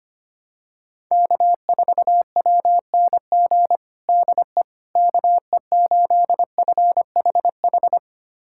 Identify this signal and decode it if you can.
Morse code, heard as a keyed tone, about 25 wpm, 700 Hz. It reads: K4WNG DE KE8FHH
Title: USA Callsign DE Callsign 25wpm